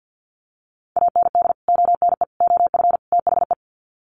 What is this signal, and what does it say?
Morse code, heard as a keyed tone, about 50 wpm, 700 Hz. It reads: AND ONE OF THE
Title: Sets of 4 Words 50wpm